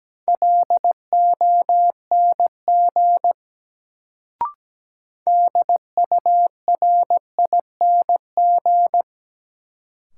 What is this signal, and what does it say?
Morse code, heard as a keyed tone, about 17 wpm, 700 Hz. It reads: LONG DURING
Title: Sets of 2 Words - New Word Drill - 17wpm